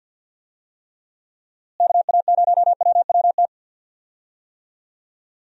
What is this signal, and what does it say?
Morse code, heard as a keyed tone, about 47 wpm, 700 Hz. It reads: KA0WWT
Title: USA Call Signs - 6 Characters - Repeated 3x - 50wpm